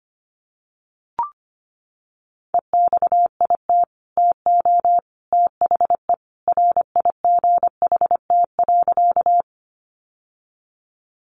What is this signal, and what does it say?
Morse code, heard as a keyed tone, about 25 wpm, 700 Hz. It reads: EXIT TO THE RIGHT.
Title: Sentences from Top 1100 Words 25wpm